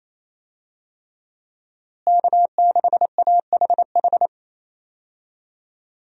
Morse code, heard as a keyed tone, about 28 wpm, 700 Hz. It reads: K6AHH
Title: USA Call Signs - 5 Characters - Repeated 3x - 28wpm